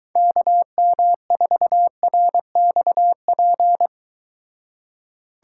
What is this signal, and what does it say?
Morse code, heard as a keyed tone, about 23 wpm, 700 Hz. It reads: KM4RXP